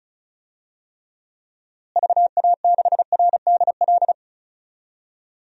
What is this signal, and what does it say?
Morse code, heard as a keyed tone, about 36 wpm, 700 Hz. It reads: VA6RDL
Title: DXCC-100 Call Signs - 6 Characters - Repeated 3x - 35wpm